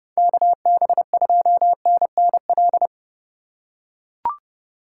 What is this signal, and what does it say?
Morse code, heard as a keyed tone, about 30 wpm, 700 Hz. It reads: KB2NNL
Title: USA Call Signs - 6 Characters - 8x Character Spacing - 30wpm